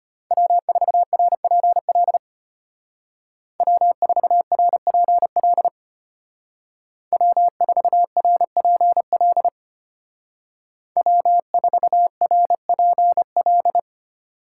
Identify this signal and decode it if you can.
Morse code, heard as a keyed tone, about 32 wpm, 700 Hz. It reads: W4RPL W4RPL W4RPL W4RPL